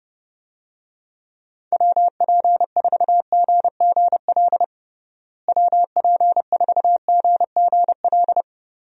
Morse code, heard as a keyed tone, about 29 wpm, 700 Hz. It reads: WP4GGL WP4GGL
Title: USA Call Signs - 6 Characters - Repeated 3x - 30wpm